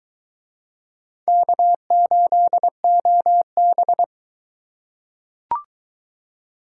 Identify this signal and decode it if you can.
Morse code, heard as a keyed tone, about 22 wpm, 700 Hz. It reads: K8OB